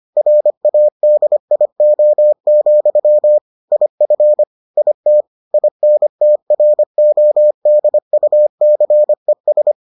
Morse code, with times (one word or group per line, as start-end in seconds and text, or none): 0.2-3.4 RADIO,
3.7-4.4 IF
4.8-5.2 IT
5.5-9.7 INTRODUCES